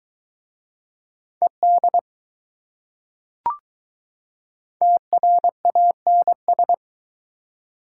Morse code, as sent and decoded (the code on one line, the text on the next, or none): . -.. / - .-. .- -. ...
ED TRANS